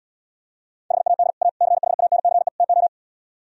Morse code, heard as a keyed tone, about 77 wpm, 700 Hz. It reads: HAD A CHANCE TO